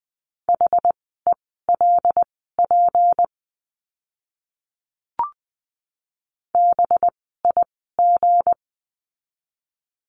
Morse code, heard as a keyed tone, about 20 wpm, spaced as for 15 wpm, 700 Hz. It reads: HELP BIG
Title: 206 Lesson 33f - Letter j - Words 20wpm